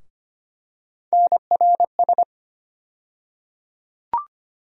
Morse code, heard as a keyed tone, about 25 wpm, 700 Hz. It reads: NRS
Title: USA Call Sign Postfixes - 3 Letters 25wpm